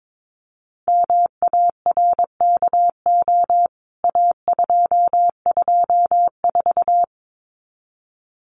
Morse code, heard as a keyed tone, about 22 wpm, 700 Hz. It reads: MARKO A224
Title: CWT Exchange 22wpm